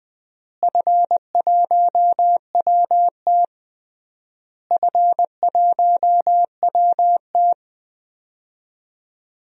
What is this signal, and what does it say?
Morse code, heard as a keyed tone, about 20 wpm, 700 Hz. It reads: F1WT F1WT
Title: DXCC-100 Call Signs - 4 Characters - Repeated 3x - 20wpm